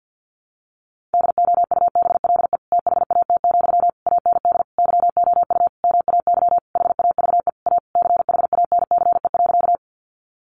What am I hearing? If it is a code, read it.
Morse code, heard as a keyed tone, about 50 wpm, 700 Hz. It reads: DOUBLE THAT, AND YOU MAY HAVE A CHANCE.
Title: Sentences from Top 1400 Words 50wpm